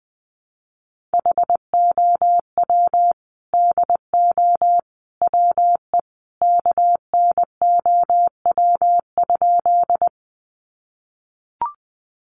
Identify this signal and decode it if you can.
Morse code, heard as a keyed tone, about 20 wpm, 700 Hz. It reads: HOW DO WE KNOW?
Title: A Sound Of Thunder 20wpm